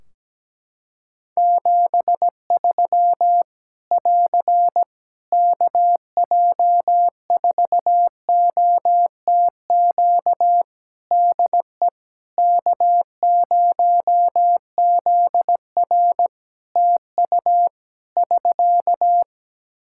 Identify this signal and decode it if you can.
Morse code, heard as a keyed tone, about 17 wpm, 700 Hz. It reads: 73 <AR> KJ4OTQ DE K0ZR TU <SK>
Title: Practice QSO - Exchange 4 - Ending - 17wpm